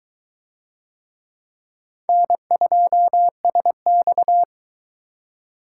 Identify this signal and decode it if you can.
Morse code, heard as a keyed tone, about 20 wpm, 700 Hz. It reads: N2SX